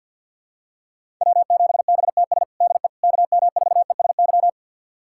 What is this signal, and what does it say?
Morse code, heard as a keyed tone, about 51 wpm, 700 Hz. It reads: W7BTS DE KM4ESY
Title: USA Callsign DE Callsign 50wpm